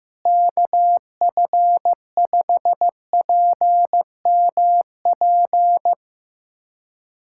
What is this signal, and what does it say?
Morse code, heard as a keyed tone, about 15 wpm, 700 Hz. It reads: KF5PMP
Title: USA Callsign DE Callsign - 6 Characters - 15wpm